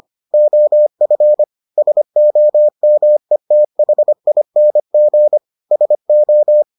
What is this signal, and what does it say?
Morse code, heard as a keyed tone, about 25 wpm, 600 Hz. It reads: OF SOMETHING SO